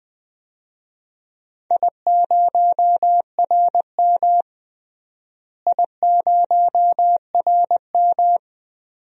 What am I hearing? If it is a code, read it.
Morse code, heard as a keyed tone, about 20 wpm, 700 Hz. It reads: I0RM I0RM
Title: DXCC-100 Call Signs - 4 Characters - Repeated 2x - 20wpm